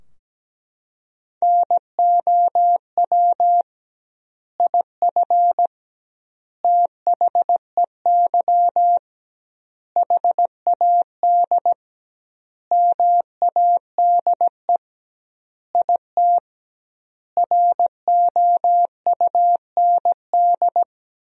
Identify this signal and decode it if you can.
Morse code, heard as a keyed tone, about 17 wpm, 700 Hz. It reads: NOW IF THEY HAD MADE IT ROUND
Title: 2x Word-Spacing - Sentences from Top 500 Words - 17wpm